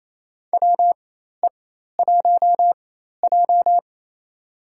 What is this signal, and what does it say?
Morse code, heard as a keyed tone, about 28 wpm, spaced as for 13 wpm, 700 Hz. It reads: WE1J
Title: USA Call Signs - 4 Characters - 4x Character Spacing - 28wpm